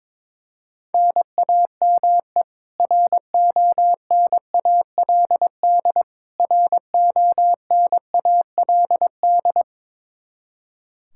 Morse code, heard as a keyed tone, about 21 wpm, 700 Hz. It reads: NAME RONALD RONALD
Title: Practice QSO - Exchange 1 - Beginning - 22wpm